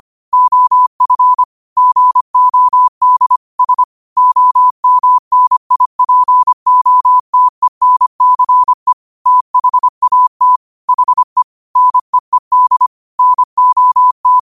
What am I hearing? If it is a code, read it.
Morse code, heard as a keyed tone, about 25 wpm, 1000 Hz. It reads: OF GOD S OMNIPOTENCE THAT HE NEED NOT
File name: qotdfiles / 25wpm